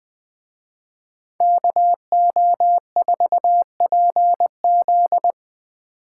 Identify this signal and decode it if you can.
Morse code, heard as a keyed tone, about 20 wpm, 700 Hz. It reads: KO4PZ
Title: Speed Racing - USA 2x2 Call Signs 17wpm